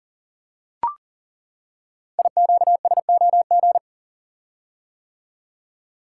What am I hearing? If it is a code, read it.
Morse code, heard as a keyed tone, about 38 wpm, 700 Hz. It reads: IQSOG
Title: Code Groups - 5 Letters - 40wpm